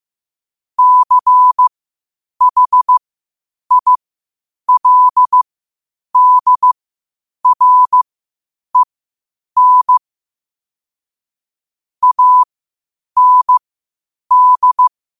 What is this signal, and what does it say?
Morse code, heard as a keyed tone, about 15 wpm, spaced as for 8 wpm, 1000 Hz. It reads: CHILDREN AND
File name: qotdfiles / 8wpm